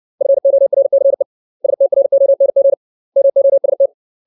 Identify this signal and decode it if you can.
Morse code, heard as a keyed tone, about 60 wpm, 550 Hz. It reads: FORCE STRONG MOST
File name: 632 words / 632 Words 60WPM